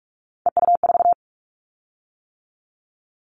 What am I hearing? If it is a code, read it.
Morse code, heard as a keyed tone, about 46 wpm, 700 Hz. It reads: EU4
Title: Callsign Prefixes for Top 50 DXCC - 45wpm